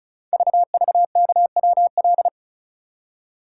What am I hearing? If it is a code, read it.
Morse code, heard as a keyed tone, about 36 wpm, 700 Hz. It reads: VVKWL
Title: Code Groups - 5 Letters - 35wpm